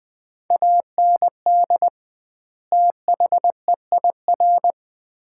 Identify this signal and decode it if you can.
Morse code, heard as a keyed tone, about 20 wpm, 700 Hz. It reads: AND THEIR
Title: 2x Word-Spacing - Sets of 2 Words - 20wpm